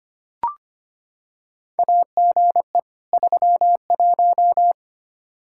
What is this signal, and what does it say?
Morse code, heard as a keyed tone, about 25 wpm, 700 Hz. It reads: AGE 31
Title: K1USN SST QSO Practice 25wpm